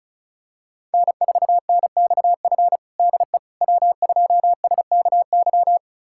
Morse code, heard as a keyed tone, about 36 wpm, 700 Hz. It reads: N4NXF DE W2SKY
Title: USA Callsign DE Callsign - 5 Characters - 35wpm